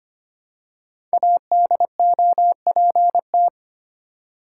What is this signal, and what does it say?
Morse code, heard as a keyed tone, about 25 wpm, 700 Hz. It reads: ADOPT
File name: 5 Letter Words - First 3 Letters Given - 25wpm